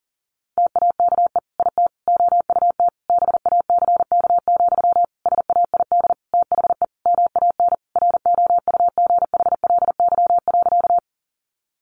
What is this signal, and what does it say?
Morse code, heard as a keyed tone, about 40 wpm, 700 Hz. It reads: TAKE IT OUT BACK, SAID THE MAN ROUGHLY.
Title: Sentences from Top 1500 Words 40wpm